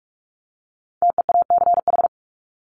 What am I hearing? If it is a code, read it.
Morse code, heard as a keyed tone, about 44 wpm, 700 Hz. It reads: TEACH
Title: ING Suffix - Top 100 Words with Base Words 45wpm